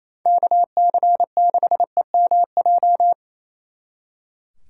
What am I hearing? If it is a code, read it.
Morse code, heard as a keyed tone, about 34 wpm, 700 Hz. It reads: KC6EMJ
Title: Speed Racing - USA 2x3 Call Signs 28wpm